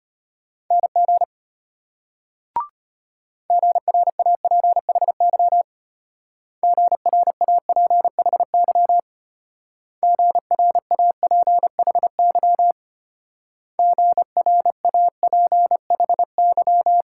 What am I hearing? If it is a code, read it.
Morse code, heard as a keyed tone, about 32 wpm, 700 Hz. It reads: NG GRAPHY GRAPHY GRAPHY GRAPHY